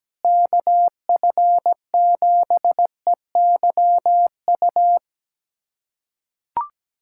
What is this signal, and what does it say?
Morse code, heard as a keyed tone, about 17 wpm, 700 Hz. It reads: KF7EYU
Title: USA Callsign DE Callsign 17wpm